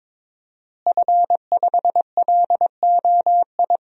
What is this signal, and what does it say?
Morse code, heard as a keyed tone, about 22 wpm, 700 Hz. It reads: F5LOI
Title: DXCC-100 Call Signs - 5 Characters - Repeated 3x - 22wpm